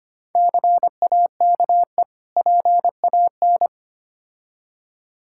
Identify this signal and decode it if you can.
Morse code, heard as a keyed tone, about 25 wpm, 700 Hz. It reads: CAKE PAN
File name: Easy Words - Speed-Racing with Hints - 25wpm